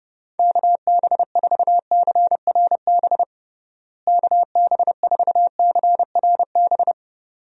Morse code, heard as a keyed tone, about 30 wpm, 700 Hz. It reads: KB4CRB KB4CRB
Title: USA Call Signs - 6 Characters - Repeated 2x - 30wpm